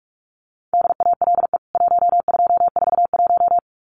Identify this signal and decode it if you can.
Morse code, heard as a keyed tone, about 44 wpm, 700 Hz. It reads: DALE 1241
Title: CWT Exchange 45wpm